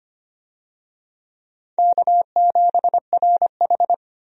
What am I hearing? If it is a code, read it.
Morse code, heard as a keyed tone, about 25 wpm, 700 Hz. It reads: K7RH
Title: USA Call Signs - 4 Characters - 25wpm